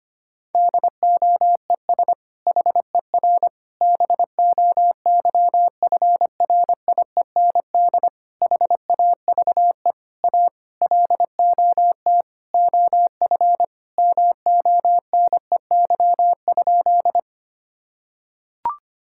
Morse code, heard as a keyed tone, about 25 wpm, 700 Hz. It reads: DOES HER BOYFRIEND HAVE A LOT OF MONEY?
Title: Sentences from Top 1000 Words - Review Part 5 - 25wpm